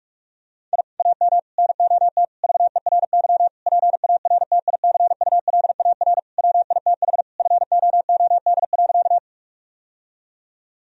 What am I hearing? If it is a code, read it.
Morse code, heard as a keyed tone, about 46 wpm, 700 Hz. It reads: I AM NOT VERY PARTICULAR WITH FOOD.
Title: Sentences from Top 1000 Words - Review Part 4 - 45wpm